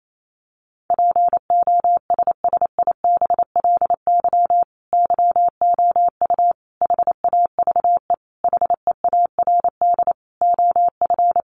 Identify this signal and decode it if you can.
Morse code, heard as a keyed tone, about 28 wpm, 700 Hz. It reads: POSSIBLY YOU HAVE HEARD OF
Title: Sentences from Top 1300 Words 28wpm